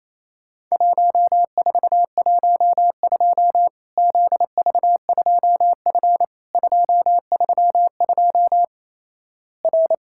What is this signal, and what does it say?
Morse code, heard as a keyed tone, about 28 wpm, 700 Hz. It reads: 1412 ZV2F 232 R
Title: Worked All Europe DX Contest - QTC Exchange - 28wpm